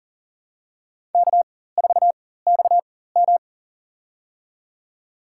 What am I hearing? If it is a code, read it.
Morse code, heard as a keyed tone, about 40 wpm, spaced as for 19 wpm, 700 Hz. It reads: K4XM
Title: USA Call Signs - 4 Characters - 4x Character Spacing - 40wpm